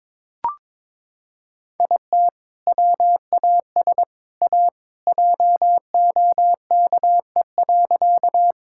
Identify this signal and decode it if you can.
Morse code, heard as a keyed tone, about 22 wpm, 700 Hz. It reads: IT WAS A JOKE.
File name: practice / 22wpm / Sentences with 3-Word Phrases - 22wpm